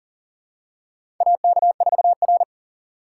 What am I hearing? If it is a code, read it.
Morse code, heard as a keyed tone, about 38 wpm, 700 Hz. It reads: AK4R
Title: USA Call Signs - 8x Character Spacing - 40wpm